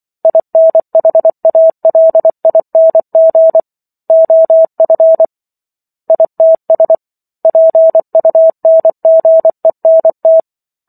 Morse code, heard as a keyed tone, about 24 wpm, 650 Hz. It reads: INHALING OF ITS PUNGENT
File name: Chapter 15 - 24 WPM